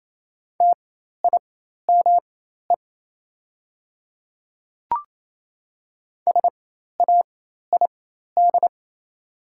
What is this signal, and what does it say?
Morse code, heard as a keyed tone, about 27 wpm, spaced as for 13 wpm, 690 Hz. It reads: TIME SAID
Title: 4x Character Spacing - Top 100 Words - 28wpm